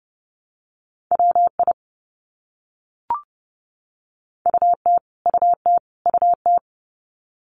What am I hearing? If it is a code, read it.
Morse code, heard as a keyed tone, about 30 wpm, 700 Hz. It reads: WI UT UT UT
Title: US States Repeated 3x 30wpm